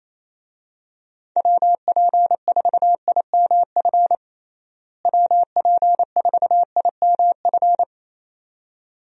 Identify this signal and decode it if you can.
Morse code, heard as a keyed tone, about 28 wpm, 700 Hz. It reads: WP4IMF WP4IMF
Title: USA Call Signs - 6 Characters - Repeated 2x - 28wpm